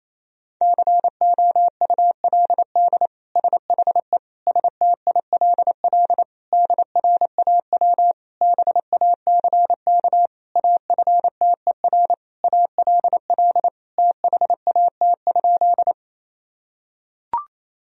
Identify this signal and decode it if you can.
Morse code, heard as a keyed tone, about 28 wpm, 700 Hz. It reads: COULD SHE STILL DRAW BACK AFTER ALL THAT?
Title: Sentences from Top 1000 Words - Review Part 1 - 28wpm